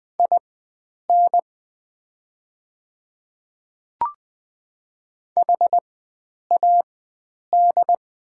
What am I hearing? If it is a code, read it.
Morse code, heard as a keyed tone, about 20 wpm, spaced as for 9 wpm, 700 Hz. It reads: IN HAD
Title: 4x Character Spacing - Top 300 Words - 20wpm